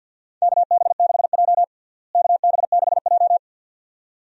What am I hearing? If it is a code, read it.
Morse code, heard as a keyed tone, about 52 wpm, 700 Hz. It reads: KB6J KB6J
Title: USA Call Signs - 4 Characters - Repeated 2x - 50wpm